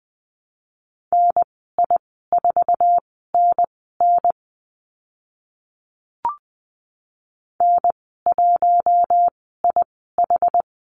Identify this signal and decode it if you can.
Morse code, heard as a keyed tone, about 20 wpm, spaced as for 15 wpm, 700 Hz. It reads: NI4NN N1IH
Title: 062 Lesson 11g - Letter h - Call Signs 20wpm